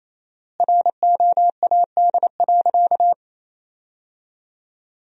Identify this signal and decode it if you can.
Morse code, heard as a keyed tone, about 28 wpm, 700 Hz. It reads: ROAD.
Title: 4x Word-Spacing - Sentences from Top 400 Words - 28wpm